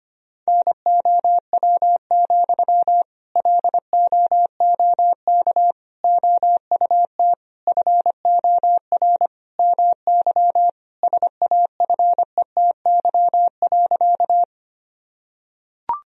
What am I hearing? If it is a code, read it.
Morse code, heard as a keyed tone, about 25 wpm, 700 Hz. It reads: NOW, LOOK OUT FOR MY SAFETY.
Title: Sentences from Top 1000 Words - Review Part 4 - 25wpm